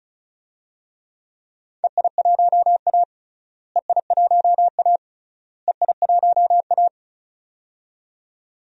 Morse code, heard as a keyed tone, about 34 wpm, 700 Hz. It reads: EI1A EI1A EI1A
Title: DXCC-100 Call Signs - 4 Characters - Repeated 3x - 35wpm